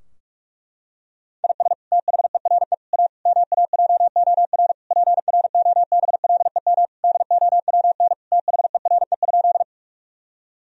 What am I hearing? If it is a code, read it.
Morse code, heard as a keyed tone, about 46 wpm, 700 Hz. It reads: IS THERE A MAJOR PROBLEM DOWN THERE?
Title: Sentences from Top 1000 Words - Review Part 4 - 45wpm